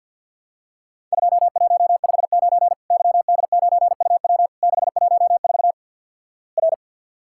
Morse code, heard as a keyed tone, about 52 wpm, 700 Hz. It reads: 2159 YD9UW 614 R
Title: Worked All Europe DX Contest - QTC Exchange with Repeats - 50wpm